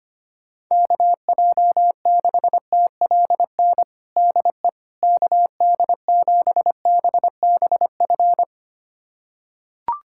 Morse code, heard as a keyed tone, about 25 wpm, 700 Hz. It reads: KJ6TLN DE KD7BBF
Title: USA Callsign DE Callsign - 6 Characters - 25wpm